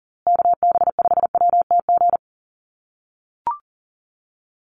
Morse code, heard as a keyed tone, about 40 wpm, 700 Hz. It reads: KB5WTG
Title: USA Call Signs Repeated 3x 40wpm